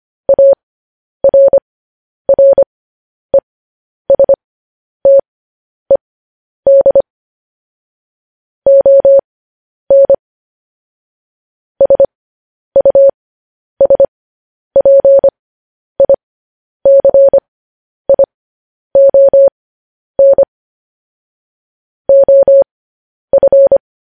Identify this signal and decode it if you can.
Morse code, heard as a keyed tone, about 25 wpm, spaced as for 10 wpm, 550 Hz. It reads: ARRESTED ON SUSPICION OF
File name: podcast / News headlines-1x-25x10